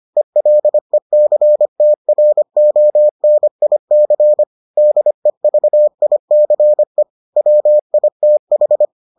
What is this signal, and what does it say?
Morse code, heard as a keyed tone, about 25 wpm, 600 Hz. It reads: ELECTRONIC DEVICE WITH